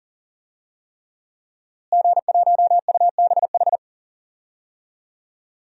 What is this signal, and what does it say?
Morse code, heard as a keyed tone, about 40 wpm, 700 Hz. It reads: G1UBH
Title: DXCC-100 Call Signs - 5 Characters - Repeated 3x - 40wpm